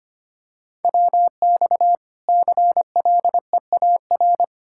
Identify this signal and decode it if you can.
Morse code, heard as a keyed tone, about 25 wpm, 700 Hz. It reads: WX CLEAR
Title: Practice QSO - Exchange 2 - Info - 25wpm